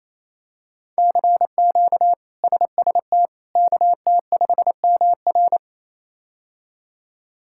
Morse code, heard as a keyed tone, about 28 wpm, 700 Hz. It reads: CQ SST KT5MR
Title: K1USN SST QSO Practice - 28wpm